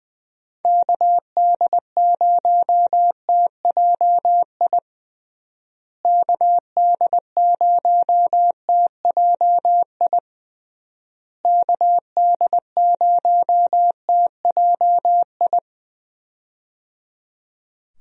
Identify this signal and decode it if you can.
Morse code, heard as a keyed tone, about 19 wpm, 700 Hz. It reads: KD0TJI KD0TJI KD0TJI
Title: USA Call Signs - 6 Characters - Repeated 3x - 20wpm